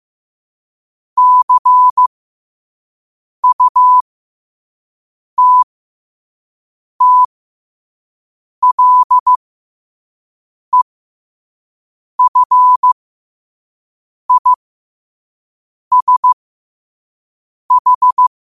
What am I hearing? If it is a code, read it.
Morse code, heard as a keyed tone, about 15 wpm, spaced as for 5 wpm, 1000 Hz. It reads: CUTTLEFISH
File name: qotdfiles / 5wpm